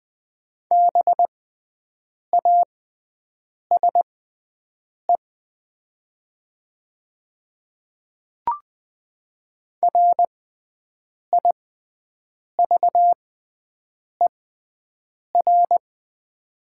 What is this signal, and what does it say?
Morse code, heard as a keyed tone, about 20 wpm, spaced as for 7 wpm, 700 Hz. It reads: BASE RIVER
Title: 6x Character Spacing - Top 300 Words - 20wpm